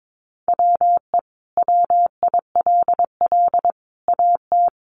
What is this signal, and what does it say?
Morse code, heard as a keyed tone, about 22 wpm, 700 Hz. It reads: WE WILL AT
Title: Sentences from Top 1200 Words 22wpm